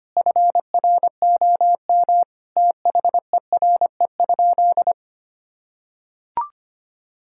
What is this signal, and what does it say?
Morse code, heard as a keyed tone, about 25 wpm, 700 Hz. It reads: FROM THERE?
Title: Sentences from Top 1000 Words - Review Part 6 - 25wpm